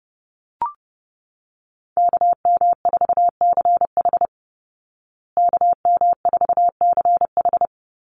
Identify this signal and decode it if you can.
Morse code, heard as a keyed tone, about 30 wpm, 700 Hz. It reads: KM4CH KM4CH
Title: Call Signs Repeated 2x - 30wpm